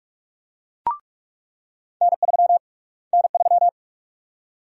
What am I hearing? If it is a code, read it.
Morse code, heard as a keyed tone, about 42 wpm, 700 Hz. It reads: N3 N3
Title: USA Call Sign Prefixes - Repeated 2x - 45wpm